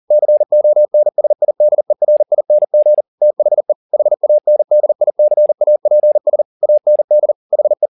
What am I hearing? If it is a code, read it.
Morse code, heard as a keyed tone, about 40 wpm, 600 Hz. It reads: CONSIDERING THE HANDICAPS AND HE
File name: the old man and the sea 2 40wpm